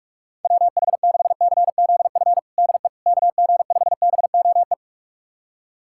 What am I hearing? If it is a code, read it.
Morse code, heard as a keyed tone, about 46 wpm, 700 Hz. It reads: WH6CZF DE KG5BOE
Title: USA Callsign DE Callsign - 6 Characters - 45wpm